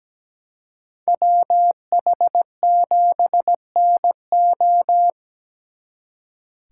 Morse code, heard as a keyed tone, about 19 wpm, 700 Hz. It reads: WH7NO
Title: Speed Racing - USA 2x2 Call Signs 17wpm